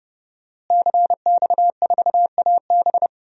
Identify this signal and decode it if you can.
Morse code, heard as a keyed tone, about 30 wpm, 700 Hz. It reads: CX4AB
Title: DXCC-100 Call Signs - 5 Characters - Repeated 3x - 30wpm